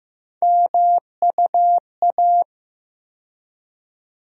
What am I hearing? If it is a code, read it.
Morse code, heard as a keyed tone, about 15 wpm, 700 Hz. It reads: MUA
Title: USA Call Sign Postfixes - 3 Letters 15wpm